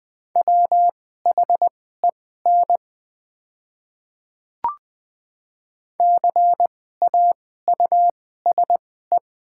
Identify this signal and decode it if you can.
Morse code, heard as a keyed tone, about 20 wpm, spaced as for 14 wpm, 700 Hz. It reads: WHEN CAUSE
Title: 2x Character Spacing - Top 200 Words - 20wpm